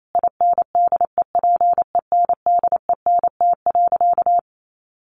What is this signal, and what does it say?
Morse code, heard as a keyed tone, about 28 wpm, 700 Hz. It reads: INDEPENDENT.
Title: Sentences from Top 900 Words 28wpm